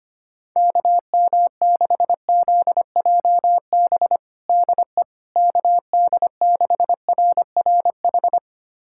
Morse code, heard as a keyed tone, about 25 wpm, 700 Hz. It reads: KM6ZJB DE KD6RRH
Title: USA Callsign DE Callsign 25wpm